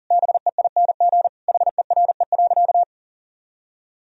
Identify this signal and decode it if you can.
Morse code, heard as a keyed tone, about 40 wpm, 700 Hz. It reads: BEING HERE.
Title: Sentences from Top 1000 Words - Review Part 2 - 40wpm